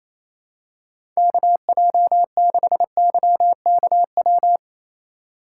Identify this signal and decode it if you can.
Morse code, heard as a keyed tone, about 27 wpm, 700 Hz. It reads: KJ6YKW
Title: USA Call Signs - 6 Characters - 2x Character Spacing - 28wpm